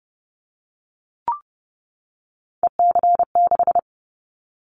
Morse code, heard as a keyed tone, about 30 wpm, 700 Hz. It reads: EC6
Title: Callsign Prefixes for Top 50 DXCC - 30wpm